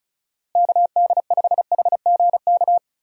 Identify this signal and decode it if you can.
Morse code, heard as a keyed tone, about 34 wpm, 700 Hz. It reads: KD5HGK
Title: USA Call Signs - 6 Characters - Repeated 3x - 35wpm